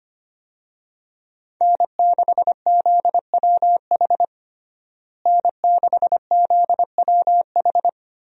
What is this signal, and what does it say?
Morse code, heard as a keyed tone, about 25 wpm, 700 Hz. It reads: N6ZWH N6ZWH
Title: USA Call Signs - 5 Characters - Repeated 2x - 25wpm